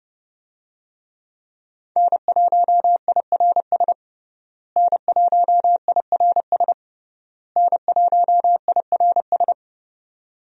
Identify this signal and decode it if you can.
Morse code, heard as a keyed tone, about 32 wpm, 700 Hz. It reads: N1IRS N1IRS N1IRS
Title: USA Call Signs - 5 Characters - Repeated 3x - 30wpm